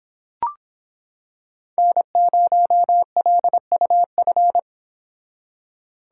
Morse code, heard as a keyed tone, about 26 wpm, 700 Hz. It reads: N0LUF